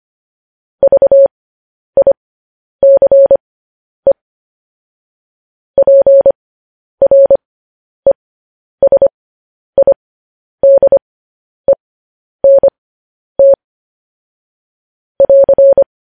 Morse code, heard as a keyed tone, about 25 wpm, spaced as for 10 wpm, 550 Hz. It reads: VICE PRESIDENT <AR>
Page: news-headlines-in-morse-code-at-10-wpm